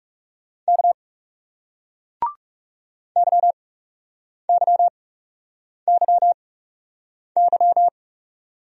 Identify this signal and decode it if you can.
Morse code, heard as a keyed tone, about 40 wpm, spaced as for 8 wpm, 700 Hz. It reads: K YYYY